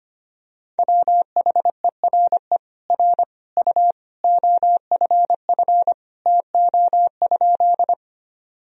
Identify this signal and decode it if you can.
Morse code, heard as a keyed tone, about 25 wpm, 700 Hz. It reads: WHERE R U OFF TO?
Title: Sentences from Top 200 Words and Abbreviations 25wpm